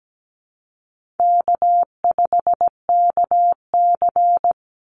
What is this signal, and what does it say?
Morse code, heard as a keyed tone, about 17 wpm, 700 Hz. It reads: K5KC
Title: Easy Call Signs - 17wpm